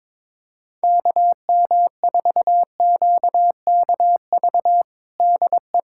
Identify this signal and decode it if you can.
Morse code, heard as a keyed tone, about 22 wpm, 700 Hz. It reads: KM4QKV DE
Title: USA Callsign DE Callsign - 6 Characters - 22wpm